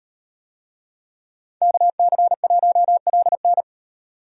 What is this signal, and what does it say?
Morse code, heard as a keyed tone, about 37 wpm, 700 Hz. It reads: KC1LN